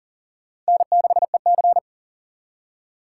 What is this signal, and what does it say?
Morse code, heard as a keyed tone, about 51 wpm, 700 Hz. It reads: N6EC